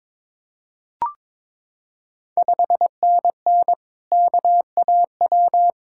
Morse code, heard as a keyed tone, about 23 wpm, 700 Hz. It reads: 5NN KAW
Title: Ontario QSO Party Exchange 22wpm